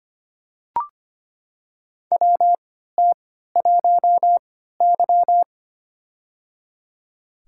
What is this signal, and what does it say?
Morse code, heard as a keyed tone, about 25 wpm, spaced as for 14 wpm, 700 Hz. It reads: WT1Y
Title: USA Call Signs - 3x Character Spacing - 25wpm